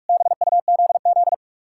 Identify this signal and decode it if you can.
Morse code, heard as a keyed tone, about 57 wpm, 700 Hz. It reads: BUZZ